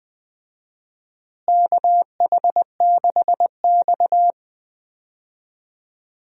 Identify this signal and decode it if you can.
Morse code, heard as a keyed tone, about 20 wpm, 700 Hz. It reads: KH6X
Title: DXCC-100 Call Signs - 4 Characters - Repeated 2x - 20wpm